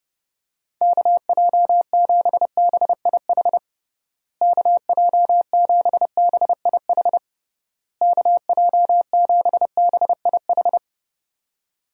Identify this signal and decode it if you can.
Morse code, heard as a keyed tone, about 32 wpm, 700 Hz. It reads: KJ7BIH KJ7BIH KJ7BIH
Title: USA Call Signs - 6 Characters - Repeated 3x - 30wpm